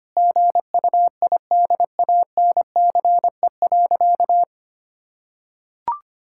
Morse code, heard as a keyed tone, about 25 wpm, 700 Hz. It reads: GUIDANCE.
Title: Sentences from Top 1600 Words 25wpm